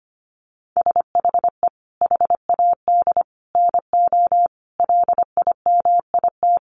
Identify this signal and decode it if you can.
Morse code, heard as a keyed tone, about 25 wpm, 700 Hz. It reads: SHE HAD NO LIMIT
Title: Sentences from Top 1500 Words 25wpm